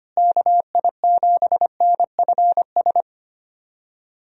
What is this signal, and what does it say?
Morse code, heard as a keyed tone, about 25 wpm, 700 Hz. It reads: KI7NFS
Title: USA Call Signs - 6 Characters - Repeated 2x - 25wpm